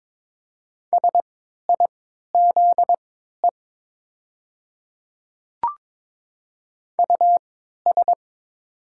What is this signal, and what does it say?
Morse code, heard as a keyed tone, about 22 wpm, spaced as for 13 wpm, 700 Hz. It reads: SIZE US
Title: 3x Character Spacing - Top 500 Words - 22wpm